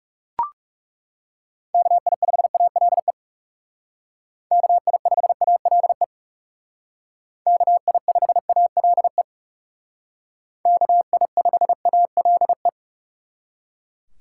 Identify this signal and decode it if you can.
Morse code, heard as a keyed tone, about 37 wpm, 700 Hz. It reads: KI5ALE KI5ALE KI5ALE KI5ALE
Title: Speed Racing - USA 2x3 Call Signs 30wpm